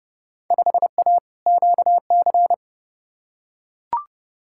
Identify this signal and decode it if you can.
Morse code, heard as a keyed tone, about 31 wpm, 700 Hz. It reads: 5A QC
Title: Field Day Exchanges Encore - 30wpm